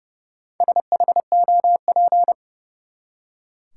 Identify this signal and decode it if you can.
Morse code, heard as a keyed tone, about 31 wpm, 700 Hz. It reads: SHOP
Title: Four Letter Words - 30wpm